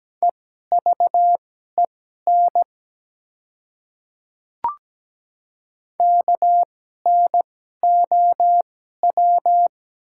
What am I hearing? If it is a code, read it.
Morse code, heard as a keyed tone, about 17 wpm, spaced as for 12 wpm, 690 Hz. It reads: EVEN KNOW
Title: 2x Character Spacing - Top 300 Words - 17wpm